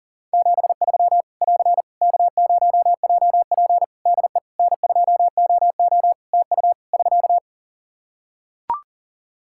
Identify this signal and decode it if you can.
Morse code, heard as a keyed tone, about 40 wpm, 700 Hz. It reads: 73 <AR> K0JP DE N2OO TU <SK>
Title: Practice QSO - Exchange 4 - Ending - 40wpm